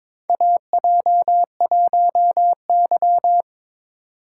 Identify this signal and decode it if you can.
Morse code, heard as a keyed tone, about 22 wpm, 700 Hz. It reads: AJ1Y
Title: USA Call Signs - 4 Characters - 3x Character Spacing - 22wpm